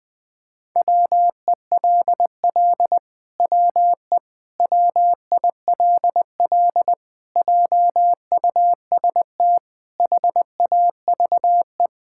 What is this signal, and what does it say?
Morse code, heard as a keyed tone, about 20 wpm, 700 Hz. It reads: WELL WE WILL JUST HAVE
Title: Sentences with 7-Word Phrases - 20wpm